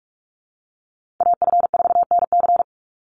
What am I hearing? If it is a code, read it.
Morse code, heard as a keyed tone, about 44 wpm, 700 Hz. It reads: AF4NC
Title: Call Signs Repeated 3x - 45wpm